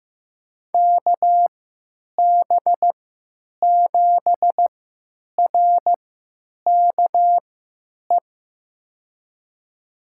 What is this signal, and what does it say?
Morse code, heard as a keyed tone, about 15 wpm, spaced as for 9 wpm, 700 Hz. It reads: KB7RKE
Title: USA Call Signs - 6 Characters - 3x Character Spacing - 15wpm